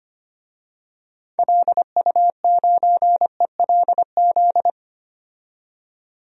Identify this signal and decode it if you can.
Morse code, heard as a keyed tone, about 25 wpm, 700 Hz. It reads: LU9ELZ
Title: DXCC-100 Call Signs - 6 Characters - 25wpm